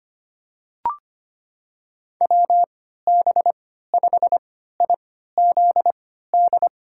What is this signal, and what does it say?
Morse code, heard as a keyed tone, about 25 wpm, spaced as for 14 wpm, 700 Hz. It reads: WB5IZD
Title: USA Call Signs - 6 Characters - 3x Character Spacing - 25wpm